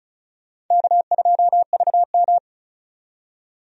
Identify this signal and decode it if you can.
Morse code, heard as a keyed tone, about 36 wpm, 700 Hz. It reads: K2VM
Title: USA Call Signs - 4 Characters - 35wpm